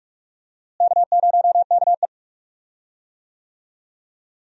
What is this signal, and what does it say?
Morse code, heard as a keyed tone, about 46 wpm, 700 Hz. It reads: K0KE
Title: USA Call Signs - 4 Characters - Repeated 2x - 45wpm